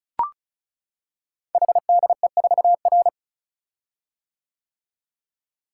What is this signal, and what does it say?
Morse code, heard as a keyed tone, about 36 wpm, 690 Hz. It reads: HDE4R
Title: Code Groups - 5 Letter:Numbers - 35wpm